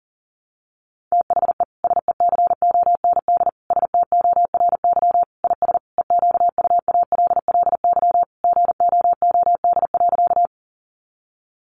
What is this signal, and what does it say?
Morse code, heard as a keyed tone, about 37 wpm, 700 Hz. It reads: THE SECOND STORY IS EQUALLY GOOD.
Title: Sentences from Top 1400 Words 40wpm